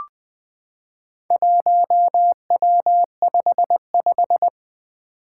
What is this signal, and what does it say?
Morse code, heard as a keyed tone, about 20 wpm, 700 Hz. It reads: 1W55
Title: Four Challenging Letter-Number Combinations 20wpm